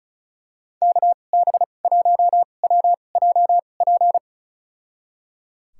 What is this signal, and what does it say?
Morse code, heard as a keyed tone, about 36 wpm, spaced as for 25 wpm, 700 Hz. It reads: KB1WJP
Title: USA Call Signs - 6 Characters - 2x Character Spacing - 35wpm